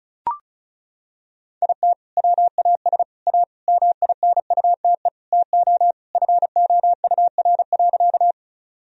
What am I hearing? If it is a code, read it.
Morse code, heard as a keyed tone, about 36 wpm, 700 Hz. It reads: IT WAS A MINUTE TO FOUR.
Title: Sentences from Top 1000 Words - Review Part 1 - 35wpm